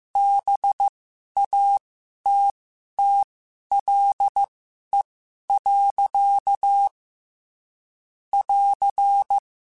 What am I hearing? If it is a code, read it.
Morse code, heard as a keyed tone, about 15 wpm, spaced as for 11 wpm, 800 Hz. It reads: BATTLE. <AR>
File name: Quotes / Quotes-mp3s / Pessimism @13